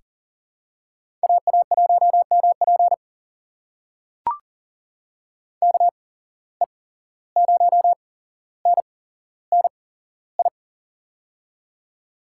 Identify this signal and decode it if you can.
Morse code, heard as a keyed tone, about 40 wpm, spaced as for 16 wpm, 700 Hz. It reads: AA1MP KE0NNI
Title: USA Call Signs - 8x Character Spacing - 40wpm